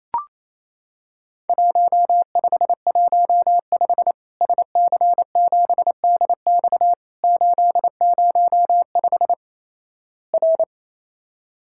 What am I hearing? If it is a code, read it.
Morse code, heard as a keyed tone, about 28 wpm, 700 Hz. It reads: 1515 SC7DX 805 R
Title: Worked All Europe DX Contest - QTC Exchange - 28wpm